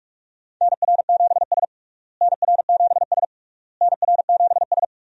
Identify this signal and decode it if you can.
Morse code, heard as a keyed tone, about 46 wpm, 700 Hz. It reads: NR7S NR7S NR7S
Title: USA Call Signs - 4 Characters - Repeated 3x - 45wpm